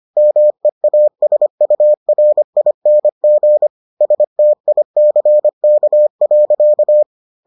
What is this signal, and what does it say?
Morse code, heard as a keyed tone, about 25 wpm, 600 Hz. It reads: MEASURING STICK.